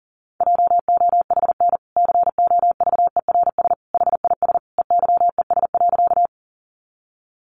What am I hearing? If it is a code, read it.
Morse code, heard as a keyed tone, about 40 wpm, 700 Hz. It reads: JOHN COVERS HIS EYES.
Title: Easy Phrases 40wpm